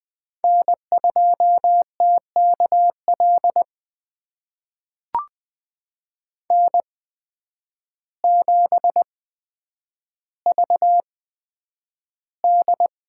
Morse code, heard as a keyed tone, about 20 wpm, spaced as for 9 wpm, 700 Hz. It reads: N2TKL N7VD
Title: USA Call Signs - 8x Character Spacing - 20wpm